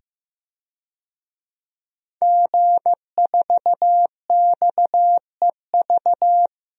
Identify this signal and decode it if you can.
Morse code, heard as a keyed tone, about 15 wpm, 700 Hz. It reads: G4XEV
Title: DXCC-100 Call Signs - 5 Characters - 15wpm